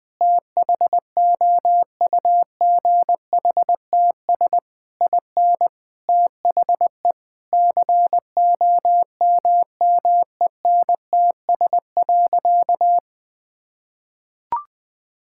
Morse code, heard as a keyed tone, about 19 wpm, 700 Hz. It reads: THOUGHTS IN THE COMMENTS.
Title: Sentences with 8-Word Phrases - 20wpm